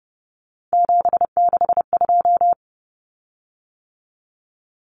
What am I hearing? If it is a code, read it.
Morse code, heard as a keyed tone, about 30 wpm, 700 Hz. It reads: 762
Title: Common Cut Numbers - 3-Digit 30wpm